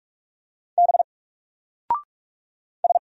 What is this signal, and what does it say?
Morse code, heard as a keyed tone, about 46 wpm, 700 Hz. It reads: B S
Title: Single Character (Letter:Number:Punctuation:Prosign) 45wpm